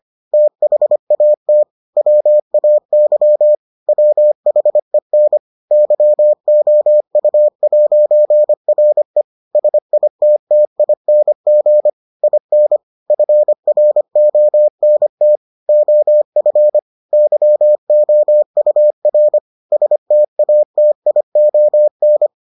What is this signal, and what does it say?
Morse code, heard as a keyed tone, about 25 wpm, 600 Hz. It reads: THAT WAY WHEN YOU'RE SITTING IN FRONT OF YOUR STATION